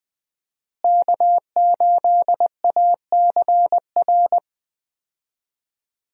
Morse code, heard as a keyed tone, about 20 wpm, 700 Hz. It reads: K8ACR
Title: USA Call Signs - 5 Characters - Repeated 2x - 20wpm